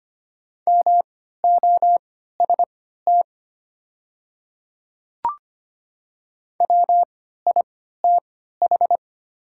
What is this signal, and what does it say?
Morse code, heard as a keyed tone, about 25 wpm, spaced as for 14 wpm, 700 Hz. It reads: MOST WITH
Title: 3x Character Spacing - Top 100 Words - 25wpm